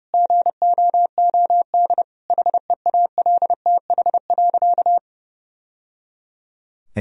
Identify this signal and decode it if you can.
Morse code, heard as a keyed tone, about 29 wpm, 700 Hz. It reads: GOOD HEALTH.
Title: Sentences from Top 1000 Words - Review Part 3 - 30wpm